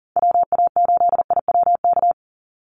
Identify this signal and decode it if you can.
Morse code, heard as a keyed tone, about 40 wpm, 700 Hz. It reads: WA8IWK
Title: Call Signs - 40wpm